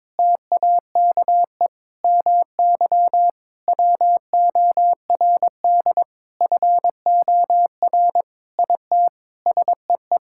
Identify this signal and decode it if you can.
Morse code, heard as a keyed tone, about 22 wpm, 700 Hz. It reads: TAKE MY WORD FOR IT SEE
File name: practice / 22wpm / Sentences with 8-Word Phrases - 22wpm